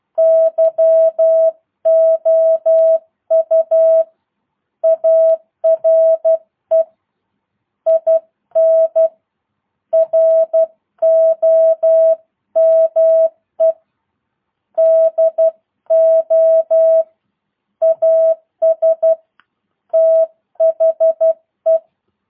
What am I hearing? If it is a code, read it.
Morse code, heard as a keyed tone, about 12 wpm, 650 Hz. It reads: YOU ARE IN ROME DO AS THE